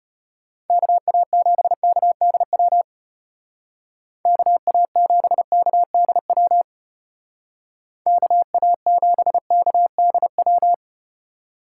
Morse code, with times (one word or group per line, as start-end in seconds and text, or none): 0.7-2.8 KA7KDW
4.3-6.6 KA7KDW
8.1-10.7 KA7KDW